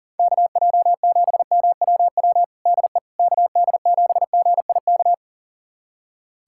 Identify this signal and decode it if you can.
Morse code, heard as a keyed tone, about 37 wpm, 700 Hz. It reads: KJ7MWW DE KD7GIK
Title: USA Callsign DE Callsign - 6 Characters - 40wpm